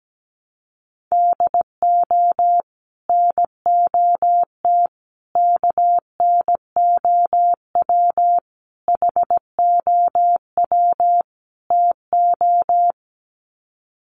Morse code, heard as a keyed tone, about 17 wpm, 700 Hz. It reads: DO NOT KNOW HOW TO
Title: Sets of 5 Words 17wpm